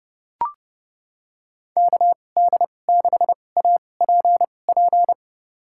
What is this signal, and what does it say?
Morse code, heard as a keyed tone, about 30 wpm, spaced as for 22 wpm, 700 Hz. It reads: KD6APP
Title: USA Call Signs - 6 Characters - 2x Character Spacing - 30wpm